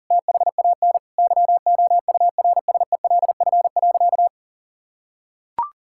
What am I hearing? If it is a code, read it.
Morse code, heard as a keyed tone, about 40 wpm, 700 Hz. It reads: THAN YOURSELF.
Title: Sentences from Top 1600 Words 40wpm